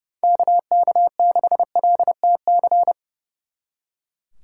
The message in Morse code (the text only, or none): KK6LTC